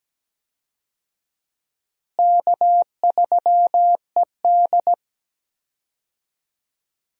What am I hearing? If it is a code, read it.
Morse code, heard as a keyed tone, about 17 wpm, 700 Hz. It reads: K3ED
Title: USA Call Signs - 4 Characters - Repeated 3x - 17wpm